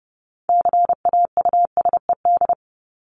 Morse code, heard as a keyed tone, about 30 wpm, 700 Hz. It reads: CAUSED
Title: ED Suffix - Top 100 Words with Base Words 30wpm